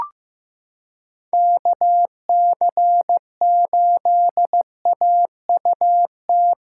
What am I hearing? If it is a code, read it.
Morse code, heard as a keyed tone, about 15 wpm, 700 Hz. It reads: KC8AUT
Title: USA Callsign DE Callsign - 6 Characters - 15wpm